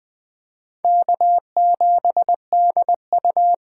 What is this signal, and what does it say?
Morse code, heard as a keyed tone, about 20 wpm, 700 Hz. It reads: K7DU
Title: USA Call Signs - 4x Character Spacing - 20wpm